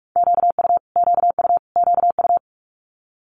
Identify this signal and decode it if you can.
Morse code, heard as a keyed tone, about 46 wpm, 700 Hz. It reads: QU QU QU
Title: Multi-letter Phonemes Repeated 3x 45wpm